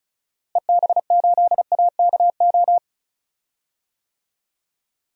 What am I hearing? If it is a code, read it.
Morse code, heard as a keyed tone, about 34 wpm, 700 Hz. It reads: EB8AKO
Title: DXCC-100 Call Signs - 6 Characters - Repeated 3x - 35wpm